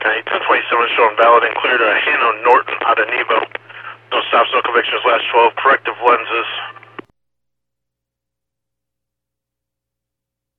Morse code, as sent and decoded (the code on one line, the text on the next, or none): none